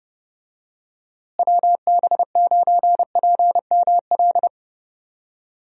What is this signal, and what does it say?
Morse code, heard as a keyed tone, about 31 wpm, 700 Hz. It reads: WB9PML
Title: USA Call Signs - 6 Characters - Repeated 2x - 30wpm